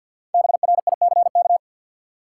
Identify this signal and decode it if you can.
Morse code, heard as a keyed tone, about 51 wpm, 700 Hz. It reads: BRICK
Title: Five Letter Words - 50wpm